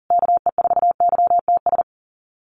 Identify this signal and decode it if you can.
Morse code, heard as a keyed tone, about 40 wpm, 700 Hz. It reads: KE4YTS
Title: USA Call Signs Repeated 3x 40wpm